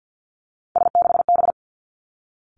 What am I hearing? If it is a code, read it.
Morse code, heard as a keyed tone, about 50 wpm, 700 Hz. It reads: S6B
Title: Three Challenging Letter-Number Combinations 50wpm